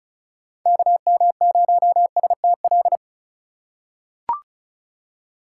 Morse code, heard as a keyed tone, about 36 wpm, 700 Hz. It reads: KM0STL
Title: USA Call Signs - 4x Character Spacing - 35wpm